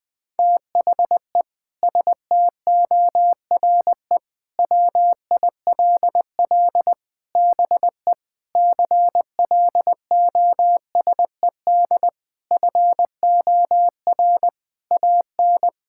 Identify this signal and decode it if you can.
Morse code, heard as a keyed tone, about 20 wpm, 700 Hz. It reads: THE STORE WILL BE CLOSED FOR AN
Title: Sentences with 6-Word Phrases - 20wpm